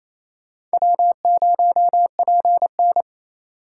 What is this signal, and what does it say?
Morse code, heard as a keyed tone, about 35 wpm, 700 Hz. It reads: W0PN
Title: Speed Racing - USA 1x2 Call Signs 28wpm